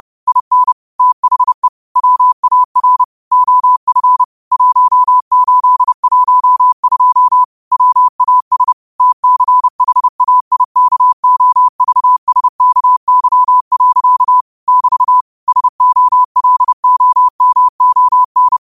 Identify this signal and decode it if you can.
Morse code, heard as a keyed tone, about 30 wpm, 1000 Hz. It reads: IN THE WAR OF 1812 WAS TCHAIKOVSKY. <BT> SOLOMON
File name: qotdfiles / 30wpm